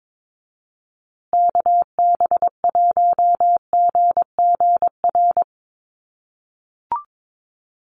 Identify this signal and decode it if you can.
Morse code, heard as a keyed tone, about 22 wpm, 700 Hz. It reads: KB1GGR
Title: USA Call Signs Repeated 3x 22wpm